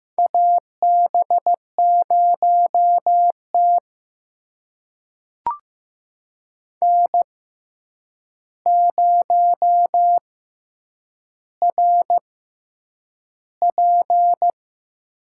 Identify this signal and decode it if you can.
Morse code, heard as a keyed tone, about 15 wpm, spaced as for 8 wpm, 700 Hz. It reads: AB0T N0RP
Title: USA Call Signs - 4 Characters - 6x Character Spacing - 15wpm